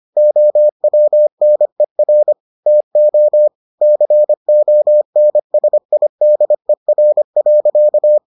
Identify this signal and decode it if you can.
Morse code, heard as a keyed tone, about 25 wpm, 600 Hz. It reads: OWNER TO CONSIDER.